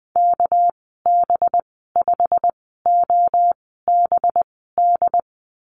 Morse code, heard as a keyed tone, about 20 wpm, spaced as for 15 wpm, 700 Hz. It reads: KB5OBD
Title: 238 Lesson 38g - Letter q - Call Signs 20wpm